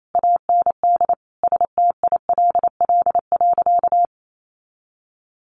Morse code, heard as a keyed tone, about 28 wpm, 700 Hz. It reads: AND STILL.
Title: Sentences from Top 400 Words 28wpm